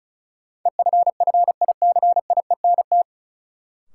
Morse code, heard as a keyed tone, about 36 wpm, 700 Hz. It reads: EFFICIENT
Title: Sentences from Top 1300 Words - New Word Drill - 35wpm